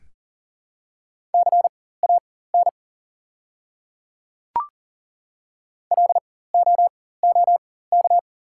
Morse code, heard as a keyed tone, about 39 wpm, spaced as for 18 wpm, 700 Hz. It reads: CAN LOOK
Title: 4x Character Spacing - Top 100 Words - 40wpm